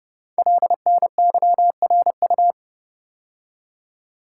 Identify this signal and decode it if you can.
Morse code, heard as a keyed tone, about 28 wpm, 700 Hz. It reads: LNYRU
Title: Code Groups - 5 Letter:Numbers - 30wpm